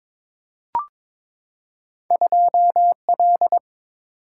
Morse code, heard as a keyed tone, about 22 wpm, 700 Hz. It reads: 2L
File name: practice / 22wpm / Two Letter-Number Combinations - 22wpm